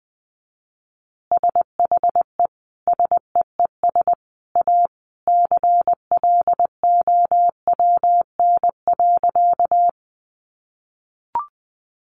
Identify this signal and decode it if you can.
Morse code, heard as a keyed tone, about 19 wpm, 700 Hz. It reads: SHE SEES A CLOWN.
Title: Easy Phrases 20wpm